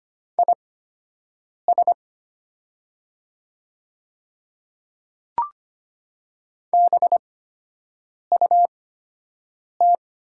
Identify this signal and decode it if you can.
Morse code, heard as a keyed tone, about 25 wpm, spaced as for 7 wpm, 700 Hz. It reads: IS BUT
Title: 8x Character Spacing - Top 100 Words - 25wpm